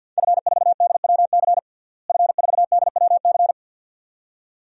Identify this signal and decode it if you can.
Morse code, heard as a keyed tone, about 52 wpm, 700 Hz. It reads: F4DWC F4DWC
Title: DXCC-100 Call Signs - 5 Characters - Repeated 2x - 50wpm